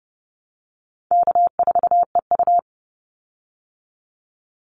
Morse code, heard as a keyed tone, about 30 wpm, 700 Hz. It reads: K4EU
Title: Easy Call Signs Repeated 2x - 30wpm